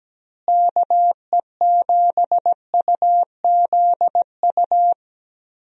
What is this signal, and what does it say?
Morse code, heard as a keyed tone, about 17 wpm, 700 Hz. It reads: KE7UZU
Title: USA Call Signs - 6 Characters - Repeated 3x - 17wpm